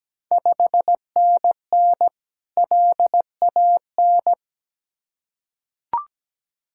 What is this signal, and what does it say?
Morse code, heard as a keyed tone, about 17 wpm, 700 Hz. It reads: 5NN LAN
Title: Ontario QSO Party Exchange 17wpm